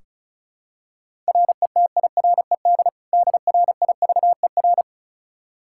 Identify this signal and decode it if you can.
Morse code, heard as a keyed tone, about 36 wpm, 700 Hz. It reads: RETIRED DRIVER
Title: K1USN SST QSO Practice 35wpm